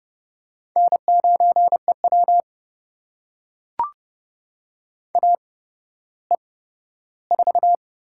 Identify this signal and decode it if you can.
Morse code, heard as a keyed tone, about 30 wpm, spaced as for 15 wpm, 700 Hz. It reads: N9EW AE4
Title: USA Call Signs - 8x Character Spacing - 30wpm